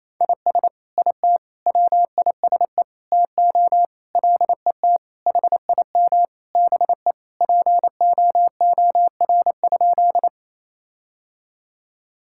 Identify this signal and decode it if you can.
Morse code, heard as a keyed tone, about 28 wpm, 700 Hz. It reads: IS IT WISE TO LET HIM BE POOR?
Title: Sentences from Top 1600 Words 28wpm